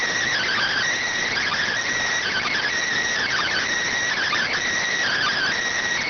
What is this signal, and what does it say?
Emergency Alert System (SAME) tones (header bursts), heard as harsh buzzing bursts. It is heard from 1.1 to 5.9 s.